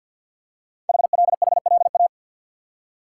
Morse code, heard as a keyed tone, about 50 wpm, 700 Hz. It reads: HLHLA
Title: Code Groups - 5 Letters - 50wpm